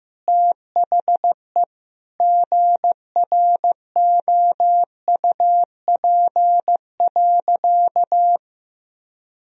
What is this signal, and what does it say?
Morse code, heard as a keyed tone, about 15 wpm, 700 Hz. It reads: THE GROUP.
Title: Sentences from Top 1000 Words - Review Part 1 - 15wpm